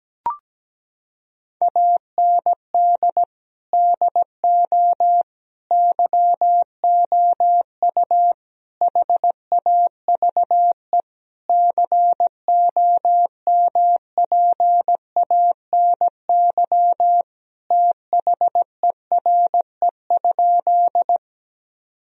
Morse code, heard as a keyed tone, about 17 wpm, 700 Hz. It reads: AND DO YOU HAVE COMPANY THERE?
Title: Sentences from Top 1000 Words - Review Part 3 - 17wpm